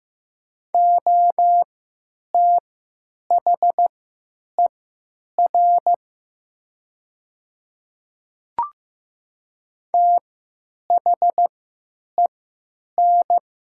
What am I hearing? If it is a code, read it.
Morse code, heard as a keyed tone, about 15 wpm, spaced as for 9 wpm, 700 Hz. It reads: OTHER THEN
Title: 3x Character Spacing - Top 200 Words - 15wpm